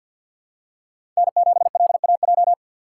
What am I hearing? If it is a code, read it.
Morse code, heard as a keyed tone, about 53 wpm, 700 Hz. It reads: N7LAJ
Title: USA Call Signs - 5 Characters - 2x Character Spacing - 50wpm